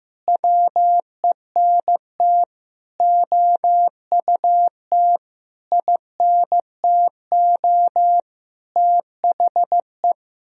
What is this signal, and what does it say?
Morse code, heard as a keyed tone, about 16 wpm, 700 Hz. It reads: WENT OUT INTO THE
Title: Sentences from Top 1000 Words - Review Part 2 - 15wpm